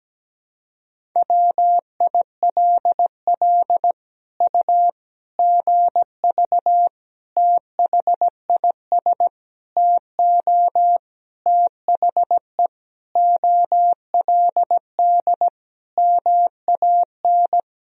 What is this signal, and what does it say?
Morse code, heard as a keyed tone, about 17 wpm, 700 Hz. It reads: WILL U GV THIS TO THE OLD MAN
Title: Sentences from Top 200 Words and Abbreviations 17wpm